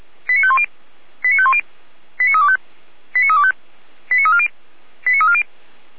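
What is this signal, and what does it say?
Five-tone selective call (ZVEI), heard as quick runs of five tones, tones of about 70 ms. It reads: 89510 89510 89325 89325 89340 89340
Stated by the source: ZVEI five-tone selective call received on an SDR, recorded by feuerwehr-orlamuende.org fire brigade.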